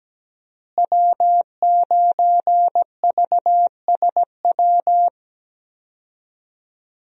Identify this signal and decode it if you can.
Morse code, heard as a keyed tone, about 17 wpm, 700 Hz. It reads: W9VSW
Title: USA Call Signs - 5 Characters - Repeated 3x - 17wpm